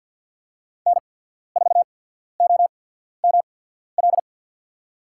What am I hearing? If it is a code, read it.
Morse code, heard as a keyed tone, about 53 wpm, spaced as for 14 wpm, 700 Hz. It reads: N4OML
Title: USA Call Signs - 5 Characters - 8x Character Spacing - 50wpm